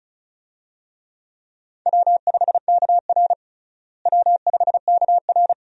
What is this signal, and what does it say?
Morse code, heard as a keyed tone, about 34 wpm, 700 Hz. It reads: W5KR W5KR
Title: USA Call Signs - 4 Characters - Repeated 2x - 35wpm